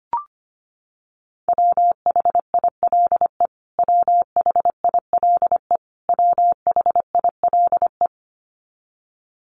Morse code, heard as a keyed tone, about 25 wpm, 700 Hz. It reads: WHILE WHILE WHILE
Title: Top 500 Words Repeated 25wpm